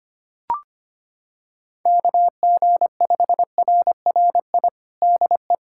Morse code, heard as a keyed tone, about 25 wpm, 700 Hz. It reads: KG5RRI DE
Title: USA Callsign DE Callsign 25wpm